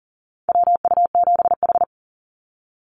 Morse code, heard as a keyed tone, about 40 wpm, 700 Hz. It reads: WU7H
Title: NA SOTA Activator Callsigns 40wpm